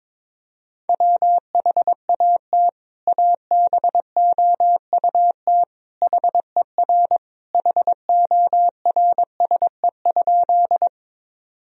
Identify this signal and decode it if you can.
Morse code, heard as a keyed tone, about 22 wpm, 700 Hz. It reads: WHAT ABOUT HER HORSE?
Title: Sentences from Top 1000 Words - Review Part 1 - 22wpm